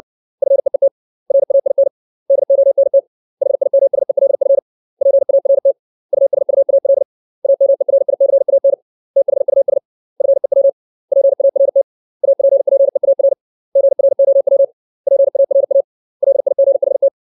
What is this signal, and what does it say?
Morse code, heard as a keyed tone, about 60 wpm, 550 Hz. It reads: FEET RAIN DONT HIMSELF PART USUAL AMERICAN THUS FEW PART APPEAR GROW PARA LIGHT
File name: 632 words / 632 Words 60WPM